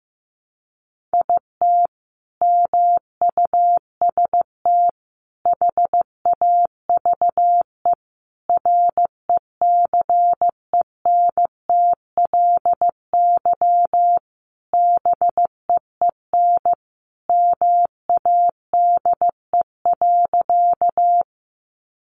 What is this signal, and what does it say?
Morse code, heard as a keyed tone, about 15 wpm, 700 Hz. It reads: IT MUST HAVE RECENTLY BEEN MADE.
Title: Sentences from Top 1200 Words 15wpm